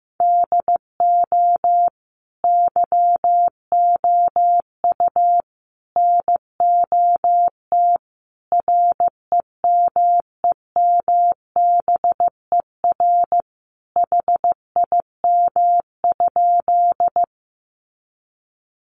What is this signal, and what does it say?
Morse code, heard as a keyed tone, about 15 wpm, 700 Hz. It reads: DO YOU NOT REMEMBER HIM?
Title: Sentences from Top 600 Words 15wpm